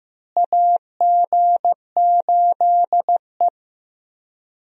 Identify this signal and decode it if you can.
Morse code, heard as a keyed tone, about 15 wpm, 700 Hz. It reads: AG8E
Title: USA Call Signs - 4 Characters - Repeated 2x - 15wpm